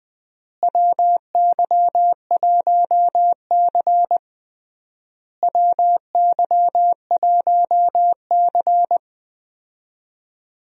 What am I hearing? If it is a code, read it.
Morse code, heard as a keyed tone, about 20 wpm, 700 Hz. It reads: WY1C WY1C
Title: USA Call Signs - 4 Characters - Repeated 3x - 20wpm